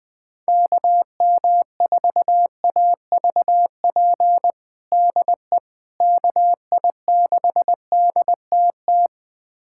Morse code, heard as a keyed tone, about 20 wpm, 700 Hz. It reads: KM4AVP DE KI6DTT
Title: USA Callsign DE Callsign - 6 Characters - 20wpm